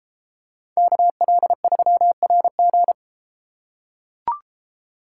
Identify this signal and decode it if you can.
Morse code, heard as a keyed tone, about 30 wpm, 700 Hz. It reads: KL3RG